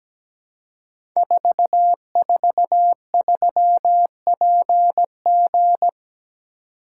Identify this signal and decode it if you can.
Morse code, heard as a keyed tone, about 17 wpm, 700 Hz. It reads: 443PG
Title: Code Groups - 5 Letter:Numbers - 17wpm